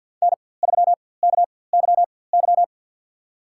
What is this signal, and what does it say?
Morse code, heard as a keyed tone, about 50 wpm, spaced as for 23 wpm, 700 Hz. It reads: N3KYY
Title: USA Call Signs - 5 Characters - 4x Character Spacing - 50wpm